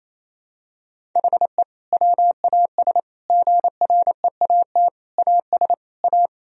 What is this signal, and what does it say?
Morse code, heard as a keyed tone, about 28 wpm, 700 Hz. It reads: HE WAS GREAT AS A
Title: Sentences from Top 1000 Words - Review Part 3 - 28wpm